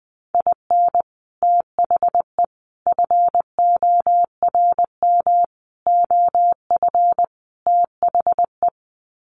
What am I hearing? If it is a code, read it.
Morse code, heard as a keyed tone, about 20 wpm, 700 Hz. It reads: IN THE FORM OF THE
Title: Sets of 5 Words Encore2 20wpm